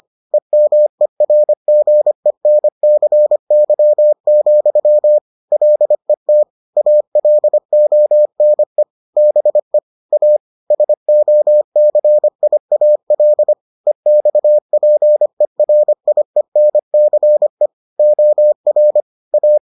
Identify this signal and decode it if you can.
Morse code, heard as a keyed tone, about 25 wpm, 600 Hz. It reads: EMERGENCY, LET ALONE BE A SOCIAL EXPERIENCE OR A